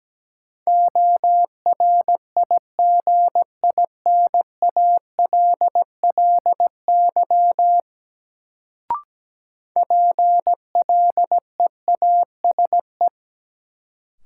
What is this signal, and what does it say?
Morse code, heard as a keyed tone, about 17 wpm, 700 Hz. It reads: ORIGINALLY PLEASE
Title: Sentences from Top 1500 Words - New Word Drill - 17wpm